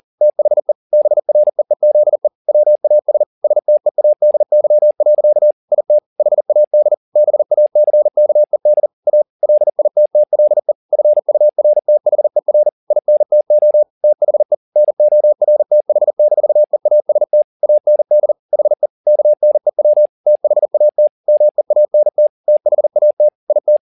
Morse code, heard as a keyed tone, about 40 wpm, 600 Hz. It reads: THE BREEZE WAS STEADY. IT HAD BACKED A LITTLE FURTHER INTO THE NORTH-EAST AND HE KNEW THAT MEANT THAT IT
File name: the old man and the sea 2 40wpm